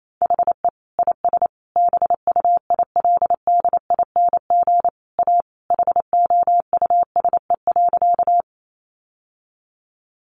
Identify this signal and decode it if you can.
Morse code, heard as a keyed tone, about 28 wpm, 700 Hz. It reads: HE IS BUILDING A HOUSE.
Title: Sentences from Top 700 Words 28wpm